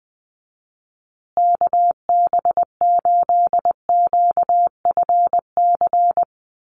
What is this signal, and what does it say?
Morse code, heard as a keyed tone, about 20 wpm, 700 Hz. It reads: KB8QFC
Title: USA Call Signs Repeated 2x 20wpm